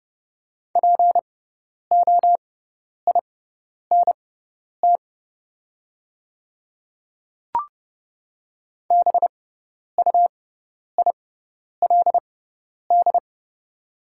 Morse code, heard as a keyed tone, about 30 wpm, spaced as for 10 wpm, 700 Hz. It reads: POINT BUILD
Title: 6x Character Spacing - Top 200 Words - 30wpm